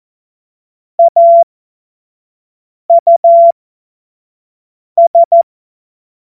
Morse code, heard as a keyed tone, about 14 wpm, spaced as for 5 wpm, 680 Hz. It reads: AUS
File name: Morsekurs / Witze / Witz 27. September 2022 25 F70